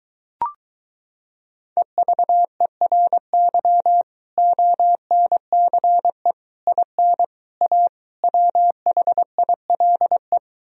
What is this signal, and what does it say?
Morse code, heard as a keyed tone, about 23 wpm, 700 Hz. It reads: EVERY ONCE IN A WHILE